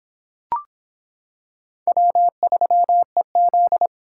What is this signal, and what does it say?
Morse code, heard as a keyed tone, about 24 wpm, 700 Hz. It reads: W3EZ